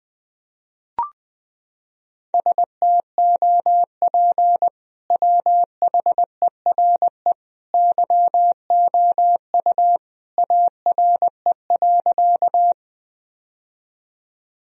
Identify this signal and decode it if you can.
Morse code, heard as a keyed tone, about 20 wpm, 700 Hz. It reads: STOP WHERE YOU ARE.
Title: Sentences from Top 1000 Words - Review Part 1 - 20wpm